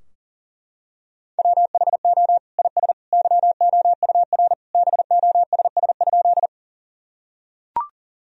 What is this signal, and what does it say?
Morse code, heard as a keyed tone, about 37 wpm, 700 Hz. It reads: WHO IS YOUR BOSS?
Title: Sentences from Top 1000 Words - Review Part 4 - 40wpm